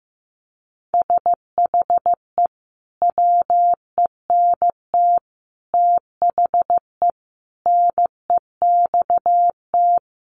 Morse code, heard as a keyed tone, about 15 wpm, 700 Hz. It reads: SHE WENT THE NEXT
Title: Sentences from Top 1500 Words 15wpm